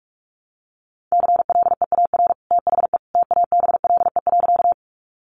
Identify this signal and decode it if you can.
Morse code, heard as a keyed tone, about 44 wpm, 700 Hz. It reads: CLEAR THE TABLE.
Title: Sentences from Top 500 Words 45wpm